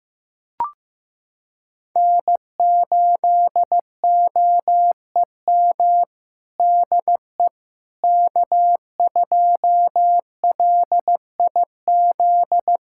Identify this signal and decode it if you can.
Morse code, heard as a keyed tone, about 15 wpm, 700 Hz. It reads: N8OEM DE K2LIZ
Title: USA Callsign DE Callsign 15wpm